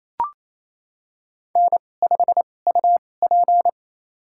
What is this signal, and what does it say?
Morse code, heard as a keyed tone, about 28 wpm, spaced as for 20 wpm, 700 Hz. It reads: N5UP
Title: USA Call Signs - 2x Character Spacing - 28wpm